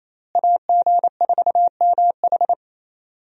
Morse code, heard as a keyed tone, about 27 wpm, 700 Hz. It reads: AG4MH
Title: USA Call Signs - 5 Characters - Repeated 3x - 28wpm